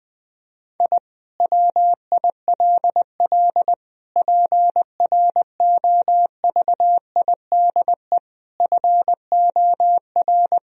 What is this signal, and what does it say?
Morse code, heard as a keyed tone, about 20 wpm, 700 Hz. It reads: I WILL PROVIDE FOR
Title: Sentences from Top 1000 Words - Review Part 3 - 20wpm